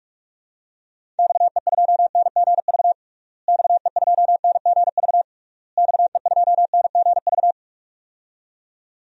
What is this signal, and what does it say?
Morse code, heard as a keyed tone, about 44 wpm, 700 Hz. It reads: XE2NGV XE2NGV XE2NGV
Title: DXCC-100 Call Signs - 6 Characters - Repeated 3x - 45wpm